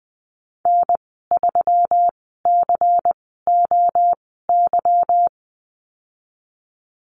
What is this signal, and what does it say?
Morse code, heard as a keyed tone, about 20 wpm, spaced as for 15 wpm, 700 Hz. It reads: N3COY
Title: 154 Lesson 25g - Letter p - Call Signs 20wpm